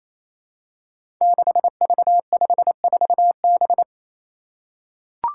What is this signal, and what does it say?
Morse code, heard as a keyed tone, about 28 wpm, 700 Hz. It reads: 6V54B
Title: Five Challenging Letter-Number Combinations 28wpm